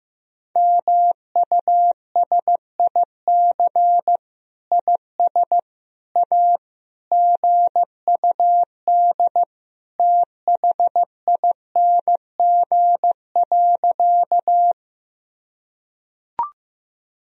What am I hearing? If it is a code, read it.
Morse code, heard as a keyed tone, about 16 wpm, 700 Hz. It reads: MUSIC IS A GUD THING.
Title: Sentences from Top 300 Words and Abbreviations 15wpm